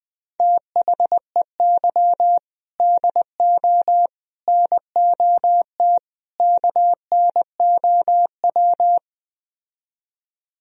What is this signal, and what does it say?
Morse code, heard as a keyed tone, about 20 wpm, 700 Hz. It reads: THEY DO NOT KNOW
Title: Speed-Racing - Sets of 4 Words 17wpm